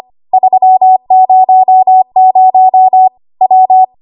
Morse code, heard as a keyed tone, about 25 wpm, 750 Hz. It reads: 300 W
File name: Archive / 25 WPM / 25WPM